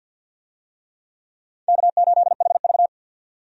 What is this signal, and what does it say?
Morse code, heard as a keyed tone, about 52 wpm, 700 Hz. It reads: K8HV
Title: USA Call Signs - 4 Characters - 50wpm